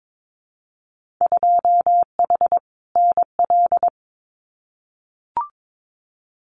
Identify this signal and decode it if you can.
Morse code, heard as a keyed tone, about 22 wpm, 700 Hz. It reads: 2H NL
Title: Winter Field Day Exchanges Encore 22wpm